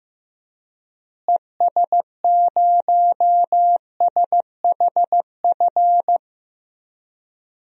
Morse code, heard as a keyed tone, about 14 wpm, 700 Hz. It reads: ES0SHF
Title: DXCC-100 Call Signs - 6 Characters - Repeated 3x - 15wpm